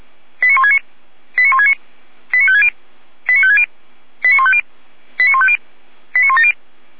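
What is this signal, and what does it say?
Five-tone selective call (ZVEI), heard as quick runs of five tones, tones of about 70 ms. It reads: 89169 89169 89560 89560 89150 89150 89170
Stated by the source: ZVEI five-tone selective call received by SDR, from feuerwehr-orlamuende.org fire brigade server.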